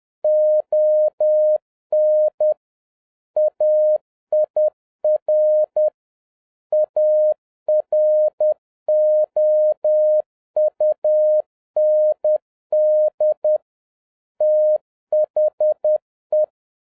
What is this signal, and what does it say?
Morse code, heard as a keyed tone, about 10 wpm, 610 Hz. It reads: ON AIR AROUND THE
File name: 2011-07-mp3 / n56 BACK FROM A BIG ACCIDENT 10wpm